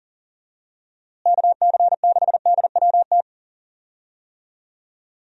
Morse code, heard as a keyed tone, about 38 wpm, 700 Hz. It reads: KC6DWT
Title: USA Call Signs - 6 Characters - Repeated 2x - 40wpm